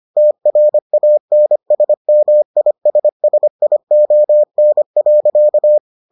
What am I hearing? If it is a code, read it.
Morse code, heard as a keyed tone, about 25 wpm, 600 Hz. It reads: TRANSMISSION.